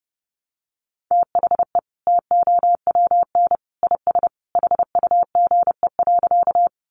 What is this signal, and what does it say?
Morse code, heard as a keyed tone, about 30 wpm, 700 Hz. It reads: THE TOWN IS HUGE.
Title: Sentences from Top 1100 Words 30wpm